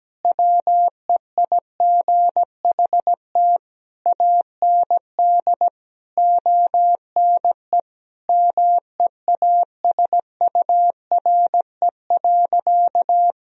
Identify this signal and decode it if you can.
Morse code, heard as a keyed tone, about 17 wpm, 700 Hz. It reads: WEIGHT AND ONE MEASURE.
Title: Sentences from Top 1000 Words - Review Part 3 - 17wpm